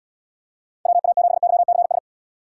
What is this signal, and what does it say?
Morse code, heard as a keyed tone, about 79 wpm, 700 Hz. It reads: WA7YCS